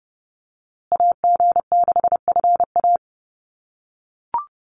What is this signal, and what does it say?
Morse code, heard as a keyed tone, about 30 wpm, 700 Hz. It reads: AG6FA
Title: USA Call Signs Repeated 3x 30wpm